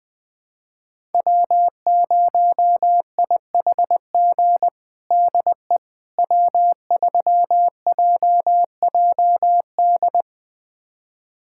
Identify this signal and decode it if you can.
Morse code, heard as a keyed tone, about 21 wpm, 700 Hz. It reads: W0IHG DE W3JJD
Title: USA Callsign DE Callsign - 5 Characters - 20wpm